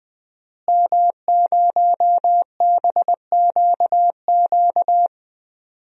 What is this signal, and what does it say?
Morse code, heard as a keyed tone, about 20 wpm, 700 Hz. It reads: M0BQQ
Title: DXCC-100 Call Signs - 5 Characters - Repeated 2x - 20wpm